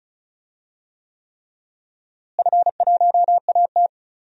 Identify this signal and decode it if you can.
Morse code, heard as a keyed tone, about 36 wpm, 700 Hz. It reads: F1AT
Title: DXCC-100 Call Signs - 4 Characters - Repeated 3x - 35wpm